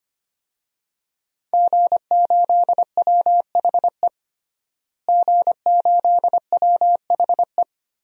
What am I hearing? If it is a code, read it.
Morse code, heard as a keyed tone, about 25 wpm, 700 Hz. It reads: G8WHE G8WHE
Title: DXCC-100 Call Signs - 5 Characters - Repeated 2x - 25wpm